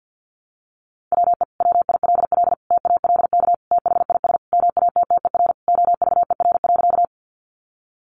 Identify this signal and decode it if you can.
Morse code, heard as a keyed tone, about 50 wpm, 700 Hz. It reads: WE WILL TALK THIS MATTER OVER.
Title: Sentences from Top 700 Words 50wpm